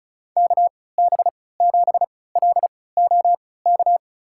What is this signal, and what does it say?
Morse code, heard as a keyed tone, about 34 wpm, spaced as for 20 wpm, 700 Hz. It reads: KB7LOK
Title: USA Call Signs - 6 Characters - 3x Character Spacing - 35wpm